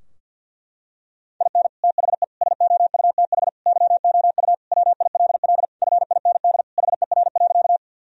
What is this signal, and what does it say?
Morse code, heard as a keyed tone, about 51 wpm, 700 Hz. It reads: IN THE SOUTH YOU WILL FIND HER.
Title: Sentences from Top 1000 Words - Review Part 2 - 50wpm